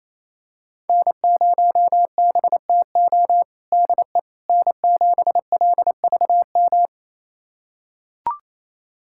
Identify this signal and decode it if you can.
Morse code, heard as a keyed tone, about 28 wpm, 700 Hz. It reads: N0BTO DE N7LVM
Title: USA Callsign DE Callsign - 5 Characters - 28wpm